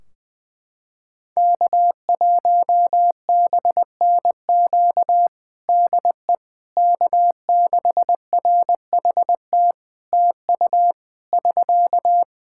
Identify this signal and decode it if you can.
Morse code, heard as a keyed tone, about 20 wpm, 700 Hz. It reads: K1BNQ DE K6RHT TU <SK>
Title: K1USN SST QSO Practice 20wpm